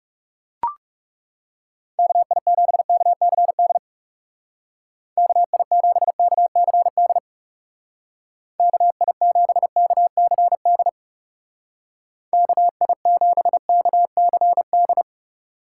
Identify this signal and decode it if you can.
Morse code, heard as a keyed tone, about 36 wpm, 700 Hz. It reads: KI7KCD KI7KCD KI7KCD KI7KCD
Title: Speed Racing - USA 2x3 Call Signs 30wpm